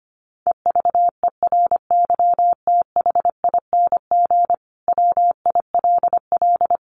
Morse code, heard as a keyed tone, about 25 wpm, 700 Hz. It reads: EVERYTHING WILL
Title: Sentences from Top 1300 Words 25wpm